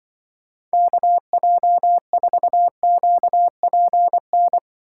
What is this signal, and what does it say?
Morse code, heard as a keyed tone, about 23 wpm, 700 Hz. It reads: KJ4QPN
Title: Speed Racing - USA 2x3 Call Signs 20wpm